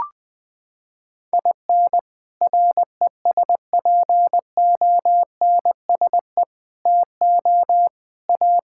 Morse code, heard as a keyed tone, about 20 wpm, 700 Hz. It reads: IN RESPONSE TO A
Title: Sentences with 6-Word Phrases - 20wpm